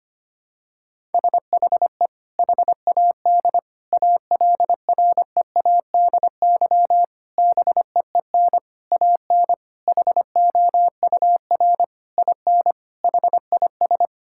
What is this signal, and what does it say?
Morse code, heard as a keyed tone, about 25 wpm, 700 Hz. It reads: SHE HAD ALREADY BEEN AN HOUR IN HIS
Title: Sentences from Top 1000 Words - Review Part 6 - 25wpm